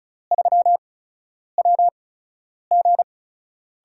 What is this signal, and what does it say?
Morse code, heard as a keyed tone, about 36 wpm, spaced as for 10 wpm, 700 Hz. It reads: 3WG
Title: USA Call Signs - 8x Character Spacing - 35wpm